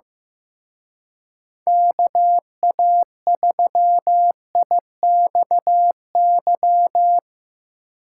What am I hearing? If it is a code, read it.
Morse code, heard as a keyed tone, about 17 wpm, 700 Hz. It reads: KA3IXY
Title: Speed Racing - USA 2x3 Call Signs 15wpm